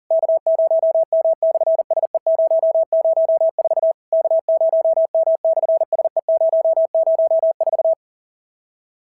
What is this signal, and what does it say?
Morse code, heard as a keyed tone, about 37 wpm, 650 Hz. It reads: K0M/SE004 K0M/SE004
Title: SOTA QSO Practice 40wpm